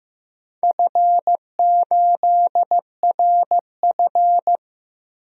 Morse code, heard as a keyed tone, about 15 wpm, 700 Hz. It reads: F8RF
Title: DXCC-100 Call Signs - 4 Characters - Repeated 3x - 15wpm